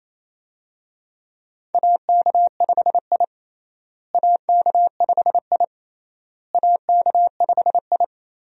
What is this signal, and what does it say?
Morse code, heard as a keyed tone, about 28 wpm, 700 Hz. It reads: AK5I AK5I AK5I
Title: USA Call Signs - 4 Characters - Repeated 3x - 28wpm